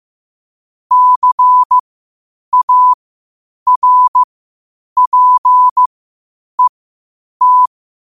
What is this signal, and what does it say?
Morse code, heard as a keyed tone, about 15 wpm, spaced as for 8 wpm, 1000 Hz. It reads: CARPET
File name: qotdfiles / 8wpm